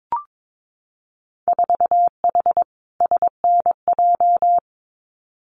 Morse code, heard as a keyed tone, about 22 wpm, 700 Hz. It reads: 4H SNJ
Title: Winter Field Day Exchanges Encore 22wpm